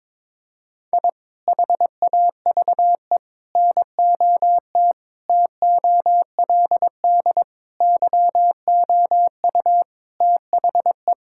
Morse code, heard as a keyed tone, about 22 wpm, 700 Hz. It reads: I HAVE NOT TOLD YOU THE
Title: Sentences from Top 1000 Words - Review Part 5 - 22wpm